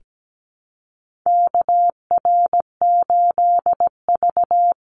Morse code, heard as a keyed tone, about 17 wpm, 700 Hz. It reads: KR8V
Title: Call Signs - 17wpm